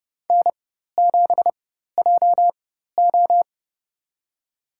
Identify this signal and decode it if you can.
Morse code, heard as a keyed tone, about 30 wpm, spaced as for 14 wpm, 700 Hz. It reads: N7JO
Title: USA Call Signs - 4 Characters - 4x Character Spacing - 30wpm